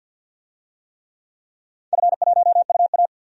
Morse code, heard as a keyed tone, about 46 wpm, 700 Hz. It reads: F1UA
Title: DXCC-100 Call Signs - 4 Characters - 50wpm